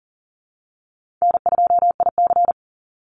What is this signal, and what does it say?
Morse code, heard as a keyed tone, about 40 wpm, 700 Hz. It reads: N2IC
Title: Easy Call Signs - 40wpm